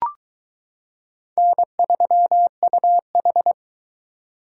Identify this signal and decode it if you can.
Morse code, heard as a keyed tone, about 22 wpm, 700 Hz. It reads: N3UH